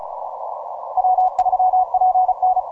Morse code, heard as a keyed tone, about 34 wpm, 740 Hz. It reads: W3PN
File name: simple-file-list / VP6R 17MCW